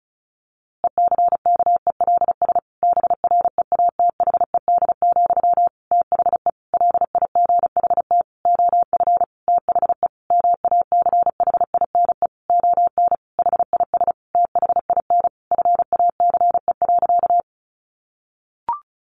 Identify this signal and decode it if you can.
Morse code, heard as a keyed tone, about 34 wpm, 700 Hz. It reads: ECKELS BREATHED, THE LIGHT OF THE MACHINE ON HIS THIN FACE.
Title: A Sound Of Thunder 35wpm